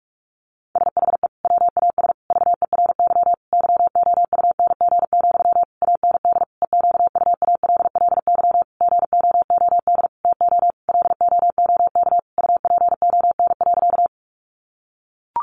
Morse code, heard as a keyed tone, about 44 wpm, 700 Hz. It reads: SHE WAS VERY YOUNG, AND EQUALLY GOOD TO LOOK UPON.
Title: Sentences from Top 1400 Words 45wpm